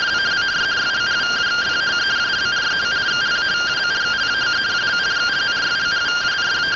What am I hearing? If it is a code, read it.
Two-tone teleprinter signal, about 170 Hz shift.